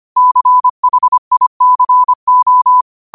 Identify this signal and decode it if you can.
Morse code, heard as a keyed tone, about 25 wpm, 1000 Hz. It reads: CHICO
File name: qotdfiles / 25wpm